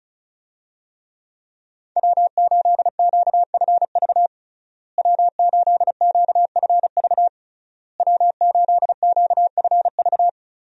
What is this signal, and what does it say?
Morse code, heard as a keyed tone, about 36 wpm, 700 Hz. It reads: W8QFV W8QFV W8QFV
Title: USA Call Signs - 5 Characters - Repeated 3x - 35wpm